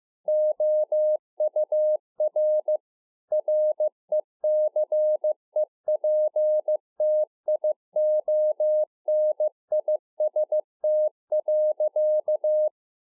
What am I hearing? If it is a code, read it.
Morse code, heard as a keyed tone, about 15 wpm, 610 Hz. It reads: OUR RECEPTIONIST.